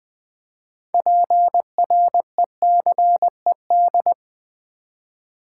Morse code, heard as a keyed tone, about 20 wpm, 700 Hz. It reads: PRECED
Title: Build A Word - 3 Letters At A Time - 9-Letter Words - 20wpm